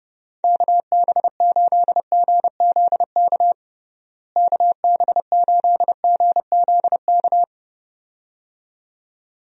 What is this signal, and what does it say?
Morse code, heard as a keyed tone, about 32 wpm, 700 Hz. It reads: KB8GZK KB8GZK
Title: USA Call Signs - 6 Characters - Repeated 2x - 30wpm